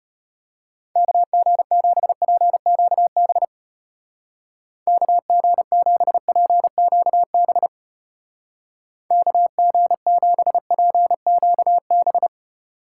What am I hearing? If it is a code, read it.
Morse code, heard as a keyed tone, about 33 wpm, 700 Hz. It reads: KG7PQB KG7PQB KG7PQB